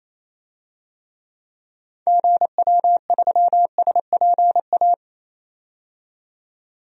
Morse code, heard as a keyed tone, about 28 wpm, 700 Hz. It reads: GW3SPA
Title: DXCC-100 Call Signs - 6 Characters - 28wpm